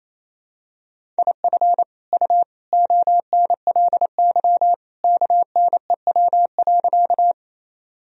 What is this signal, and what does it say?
Morse code, heard as a keyed tone, about 27 wpm, 700 Hz. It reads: IF U ONLY KNEW.
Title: Sentences from Top 400 Words and Abbreviations 28wpm